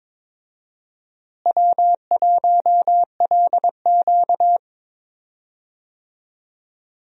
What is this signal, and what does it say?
Morse code, heard as a keyed tone, about 22 wpm, 700 Hz. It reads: W1LQ
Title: USA Call Signs - 4 Characters - Repeated 3x - 22wpm